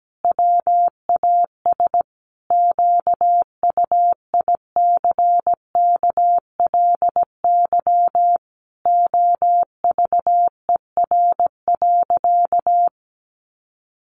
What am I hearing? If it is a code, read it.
Morse code, heard as a keyed tone, about 17 wpm, 700 Hz. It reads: WAS QUICKLY OVER.
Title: Sentences from Top 1100 Words 17wpm